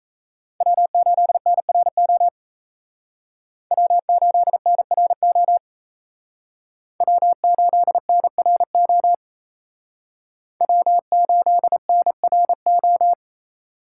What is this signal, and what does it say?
Morse code, heard as a keyed tone, about 36 wpm, 700 Hz. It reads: W8NRO W8NRO W8NRO W8NRO